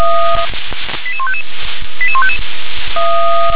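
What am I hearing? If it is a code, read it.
Five-tone selective call (ZVEI), heard as quick runs of five tones, tones of about 70 ms. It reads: 88155 88155
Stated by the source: ZVEI five-tone selective call received by SDR.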